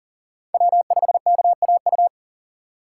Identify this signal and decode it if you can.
Morse code, heard as a keyed tone, about 39 wpm, 700 Hz. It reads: W5KAU
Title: USA Call Signs - 5 Characters - 3x Character Spacing - 40wpm